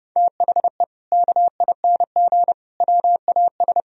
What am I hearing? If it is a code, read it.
Morse code, heard as a keyed tone, about 30 wpm, 700 Hz. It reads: THE KING WAS
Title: Sentences from Top 1000 Words - Review Part 3 - 30wpm